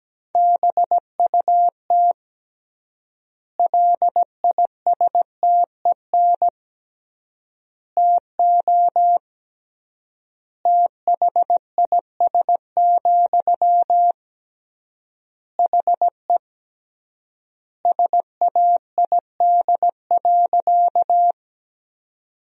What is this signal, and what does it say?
Morse code, heard as a keyed tone, about 17 wpm, 700 Hz. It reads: BUT LISTEN TO THIS, HE SAID.
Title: 3x Word-Spacing - Sentences from Top 400 Words - 17wpm